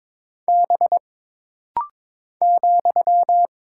Morse code, heard as a keyed tone, about 22 wpm, 700 Hz. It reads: B ,
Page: This rapid-fire timing is geared to push your ICR (Instant Character Recognition) skills.